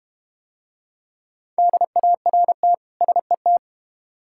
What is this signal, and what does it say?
Morse code, heard as a keyed tone, about 32 wpm, 700 Hz. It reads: DART SET